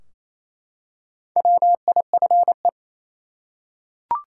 Morse code, heard as a keyed tone, about 28 wpm, 700 Hz. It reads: WIFE
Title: Four Letter Words - 28wpm